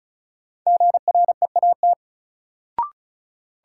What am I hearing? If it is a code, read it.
Morse code, heard as a keyed tone, about 36 wpm, 700 Hz. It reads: GREAT
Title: Sentences from Top 200 Words - New Word Drill - 35wpm